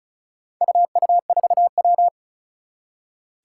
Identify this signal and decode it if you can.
Morse code, heard as a keyed tone, about 36 wpm, 690 Hz. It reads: UU4W
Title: Four Challenging Letter-Number Combinations 35wpm